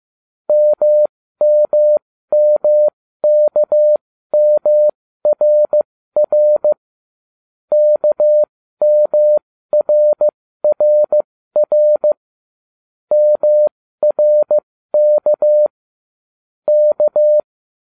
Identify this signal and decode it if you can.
Morse code, heard as a keyed tone, about 15 wpm, spaced as for 13 wpm, 600 Hz. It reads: MMMKMRR KMRRR MRK K